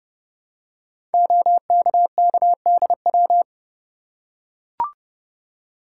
Morse code, heard as a keyed tone, about 30 wpm, 700 Hz. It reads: OKKDW
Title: Code Groups - 5 Letter:Numbers - 30wpm